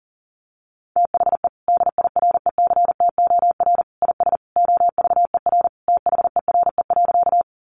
Morse code, heard as a keyed tone, about 40 wpm, 700 Hz. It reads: THE DIRECTOR IS OVER THERE.
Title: Sentences from Top 1300 Words 40wpm